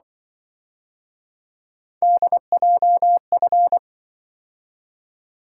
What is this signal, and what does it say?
Morse code, heard as a keyed tone, about 24 wpm, 700 Hz. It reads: DJF
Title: Speed-Racing - Three Letter-Number Combinations - 20wpm